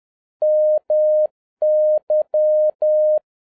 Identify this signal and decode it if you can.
Morse code, heard as a keyed tone, about 10 wpm, 610 Hz. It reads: MY
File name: 2011-07-mp3 / n41 EXCITING QRP CW OPERATION 10wpm